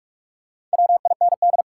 Morse code, heard as a keyed tone, about 46 wpm, 700 Hz. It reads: WIND
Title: Sentences from Top 400 Words - New Word Drill - 45wpm